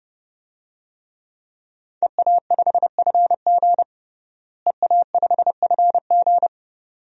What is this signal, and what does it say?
Morse code, heard as a keyed tone, about 30 wpm, 700 Hz. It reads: EA5FG EA5FG
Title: DXCC-100 Call Signs - 5 Characters - Repeated 2x - 30wpm